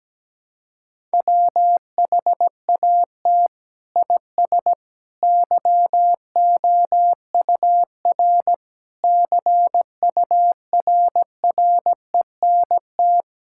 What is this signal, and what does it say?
Morse code, heard as a keyed tone, about 17 wpm, 700 Hz. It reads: WHAT IS YOUR CURRENT
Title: Sentences from Top 1000 Words - Review Part 6 - 17wpm